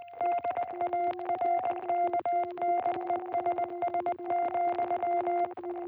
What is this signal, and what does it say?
Morse code, heard as a keyed tone, about 20 wpm, 660 Hz. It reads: DARKNESS,